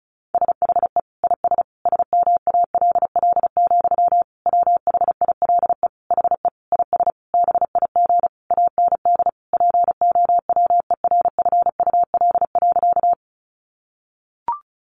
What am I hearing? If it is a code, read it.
Morse code, heard as a keyed tone, about 34 wpm, 700 Hz. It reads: SHE IS SMALL, WHILE HE IS BIG AND POWERFUL.
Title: Sentences from Top 1300 Words 35wpm